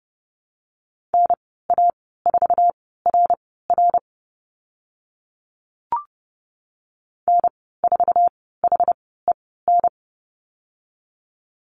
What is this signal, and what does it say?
Morse code, heard as a keyed tone, about 30 wpm, spaced as for 17 wpm, 700 Hz. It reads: NA4RR N4HEN
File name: Lesson 15g - Number 5 - Call Signs 30wpm